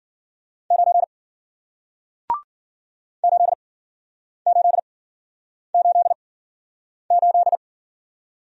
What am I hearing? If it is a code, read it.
Morse code, heard as a keyed tone, about 53 wpm, spaced as for 9 wpm, 700 Hz. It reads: 9 8888